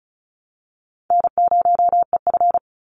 Morse code, heard as a keyed tone, about 34 wpm, 700 Hz. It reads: N0EF
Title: Call Signs - 35wpm